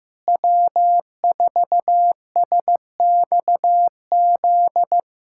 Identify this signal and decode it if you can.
Morse code, heard as a keyed tone, about 15 wpm, 700 Hz. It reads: W4SXZ
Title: USA Callsign DE Callsign - 5 Characters - 15wpm